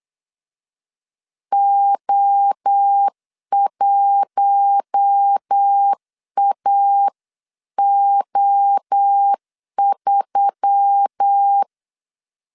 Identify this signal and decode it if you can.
Morse code, heard as a keyed tone, about 8 wpm, 800 Hz. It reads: O1AO3
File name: CW / DET / 1 6 1500C S B 8wpm 5142